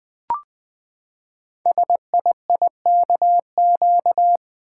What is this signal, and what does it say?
Morse code, heard as a keyed tone, about 20 wpm, 700 Hz. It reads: SIIKQ
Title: Code Groups - 5 Letters - 20wpm